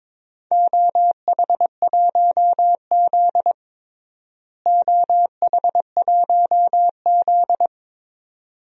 Morse code, heard as a keyed tone, about 22 wpm, 700 Hz. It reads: OH1Z OH1Z
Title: DXCC-100 Call Signs - 4 Characters - Repeated 2x - 22wpm